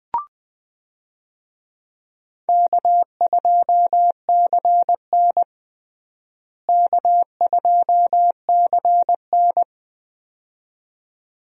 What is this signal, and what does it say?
Morse code, heard as a keyed tone, about 20 wpm, 700 Hz. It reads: K2CN K2CN
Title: USA Call Signs - 4 Characters - Repeated 2x - 20wpm